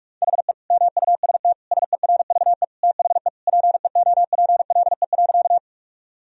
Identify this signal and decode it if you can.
Morse code, heard as a keyed tone, about 41 wpm, 700 Hz. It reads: HE MUST SERVE THE PEOPLE.
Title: Sentences from Top 1000 Words - Review Part 2 - 45wpm